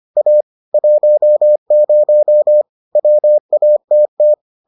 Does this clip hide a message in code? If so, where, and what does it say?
Yes: Morse code from 0.2 to 4.3 s, heard as a keyed tone, about 25 wpm, 600 Hz. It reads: A 10 WATT